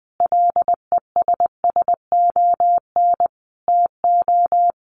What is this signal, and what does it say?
Morse code, heard as a keyed tone, about 20 wpm, 700 Hz. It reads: LESSON TO
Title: Sentences from Top 1200 Words 20wpm